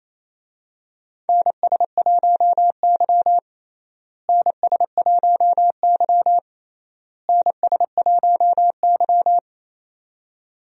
Morse code, heard as a keyed tone, about 28 wpm, 700 Hz. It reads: NS1Y NS1Y NS1Y
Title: USA Call Signs - 4 Characters - Repeated 3x - 28wpm